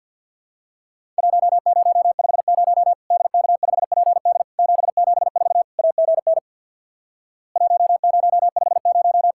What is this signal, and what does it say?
Morse code, heard as a keyed tone, about 50 wpm, 700 Hz. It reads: 1050 DK5PD 774 AGN 1050
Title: Worked All Europe DX Contest - QTC Exchange with Repeats - 50wpm